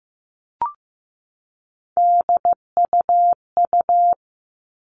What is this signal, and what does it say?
Morse code, heard as a keyed tone, about 15 wpm, 700 Hz. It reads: DUU
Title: All Cut Numbers - 3-Digit 15wpm